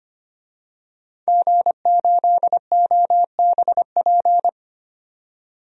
Morse code, heard as a keyed tone, about 25 wpm, 700 Hz. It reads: G8OBP
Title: DXCC-100 Call Signs - 5 Characters - 25wpm